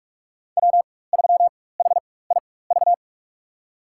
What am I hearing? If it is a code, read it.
Morse code, heard as a keyed tone, about 46 wpm, spaced as for 21 wpm, 700 Hz. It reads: W3HIV
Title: USA Call Signs - 5 Characters - 4x Character Spacing - 45wpm